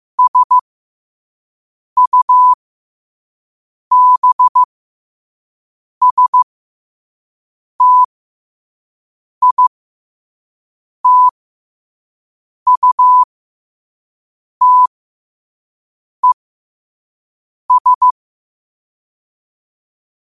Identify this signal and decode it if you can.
Morse code, heard as a keyed tone, about 15 wpm, spaced as for 5 wpm, 1000 Hz. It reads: SUBSTITUTES
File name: qotdfiles / 5wpm